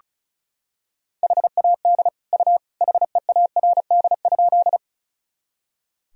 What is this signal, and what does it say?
Morse code, heard as a keyed tone, about 34 wpm, 700 Hz. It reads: HAD U HEARD?
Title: Sentences from Top 400 Words and Abbreviations 35wpm